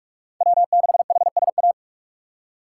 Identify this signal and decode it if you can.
Morse code, heard as a keyed tone, about 47 wpm, 700 Hz. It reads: W6HSA
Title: USA Call Signs - 5 Characters - 4x Character Spacing - 45wpm